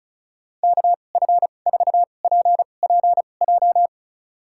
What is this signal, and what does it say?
Morse code, heard as a keyed tone, about 36 wpm, spaced as for 25 wpm, 700 Hz. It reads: KF4PPJ
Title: USA Call Signs - 6 Characters - 2x Character Spacing - 35wpm